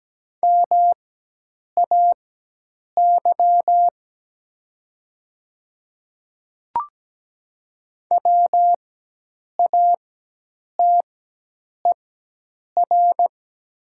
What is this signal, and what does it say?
Morse code, heard as a keyed tone, about 17 wpm, spaced as for 8 wpm, 700 Hz. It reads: MAY WATER
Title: 4x Character Spacing - Top 100 Words - 17wpm